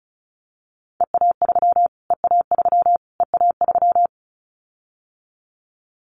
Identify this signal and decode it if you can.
Morse code, heard as a keyed tone, about 34 wpm, 700 Hz. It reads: EA3 EA3 EA3
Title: Callsign Prefixes for Top 25 DXCC - Repeated 3x - 35wpm